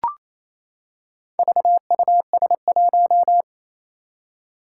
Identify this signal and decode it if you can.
Morse code, heard as a keyed tone, about 28 wpm, 700 Hz. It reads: VUS1
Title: Four Challenging Letter-Number Combinations 28wpm